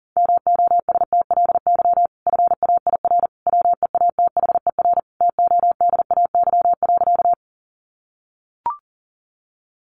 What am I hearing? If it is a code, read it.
Morse code, heard as a keyed tone, about 40 wpm, 700 Hz. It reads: MOSTLY FAIR WEATHER TODAY.
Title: Sentences from Top 1300 Words 40wpm